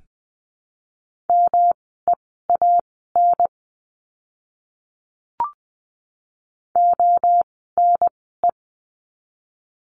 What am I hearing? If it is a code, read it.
Morse code, heard as a keyed tone, about 20 wpm, spaced as for 15 wpm, 700 Hz. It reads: MEAN ONE
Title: 237 Lesson 38f - Letter q - Words 20wpm